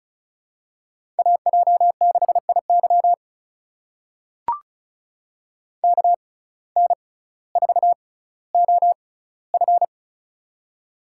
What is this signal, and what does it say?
Morse code, heard as a keyed tone, about 34 wpm, spaced as for 18 wpm, 700 Hz. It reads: AJ6IY KN4OF
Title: USA Call Signs - 5 Characters - 6x Character Spacing - 35wpm